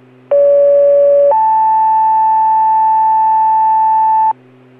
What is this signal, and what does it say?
Two-tone page, heard as a one-second tone, then a three-second tone, about 570 then 880 Hz, with noise. It is heard from 0.3 to 4.3 s.